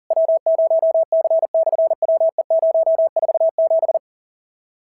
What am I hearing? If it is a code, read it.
Morse code, heard as a keyed tone, about 39 wpm, 650 Hz. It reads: W0C/WE047
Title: SOTA QSO Practice 40wpm